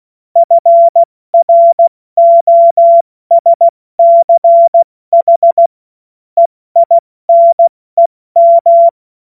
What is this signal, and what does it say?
Morse code, heard as a keyed tone, about 16 wpm, spaced as for 14 wpm, 680 Hz. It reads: FROSCH EINEM
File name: Morsekurs / Witze / Witz 08. November 2022 70 F80